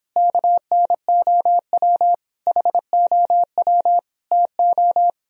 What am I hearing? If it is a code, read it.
Morse code, heard as a keyed tone, about 31 wpm, 700 Hz. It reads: KNOW HOW TO